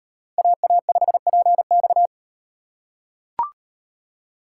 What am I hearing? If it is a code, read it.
Morse code, heard as a keyed tone, about 38 wpm, 700 Hz. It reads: AA5PX